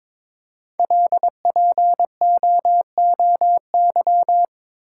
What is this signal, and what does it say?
Morse code, heard as a keyed tone, about 22 wpm, 690 Hz. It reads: LPOOY
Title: Code Groups - 5 Letters - 22wpm